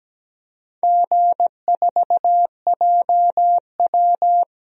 Morse code, heard as a keyed tone, about 17 wpm, 700 Hz. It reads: G4JW
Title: DXCC-100 Call Signs - 4 Characters - 17wpm